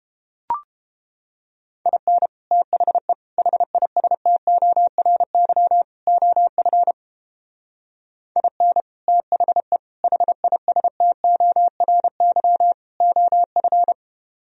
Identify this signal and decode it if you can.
Morse code, heard as a keyed tone, about 29 wpm, 700 Hz. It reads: IN THE HISTORY OF IN THE HISTORY OF